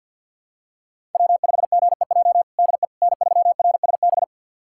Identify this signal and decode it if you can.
Morse code, heard as a keyed tone, about 52 wpm, 700 Hz. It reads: W5GEJ DE N3RSB
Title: USA Callsign DE Callsign - 5 Characters - 50wpm